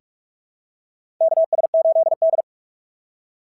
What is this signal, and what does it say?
Morse code, heard as a keyed tone, about 45 wpm, 690 Hz. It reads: KS8D
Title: SOTA QSO Practice 45wpm